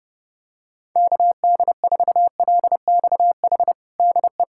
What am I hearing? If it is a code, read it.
Morse code, heard as a keyed tone, about 30 wpm, 700 Hz. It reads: KD4LXH DE
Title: USA Callsign DE Callsign 30wpm